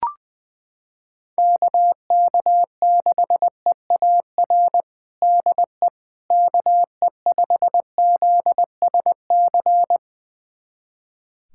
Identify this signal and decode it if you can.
Morse code, heard as a keyed tone, about 20 wpm, 700 Hz. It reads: KK6EAR DE KE5ZSC
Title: USA Callsign DE Callsign 20wpm